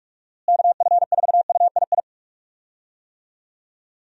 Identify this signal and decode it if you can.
Morse code, heard as a keyed tone, about 46 wpm, 700 Hz. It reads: KF4UII
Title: USA Call Signs - 6 Characters - Repeated 2x - 45wpm